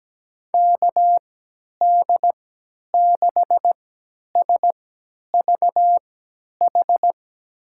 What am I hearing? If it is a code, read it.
Morse code, heard as a keyed tone, about 17 wpm, spaced as for 10 wpm, 700 Hz. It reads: KD6SVH
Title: USA Call Signs - 6 Characters - 3x Character Spacing - 17wpm